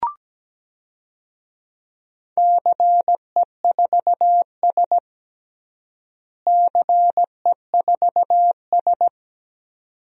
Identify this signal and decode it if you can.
Morse code, heard as a keyed tone, about 17 wpm, 700 Hz. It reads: CE4S CE4S
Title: DXCC-100 Call Signs - 4 Characters - Repeated 2x - 17wpm